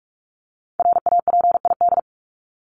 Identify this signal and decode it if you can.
Morse code, heard as a keyed tone, about 44 wpm, 700 Hz. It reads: RAPID
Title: LY Suffix - Top 100 Words with Base Words 45wpm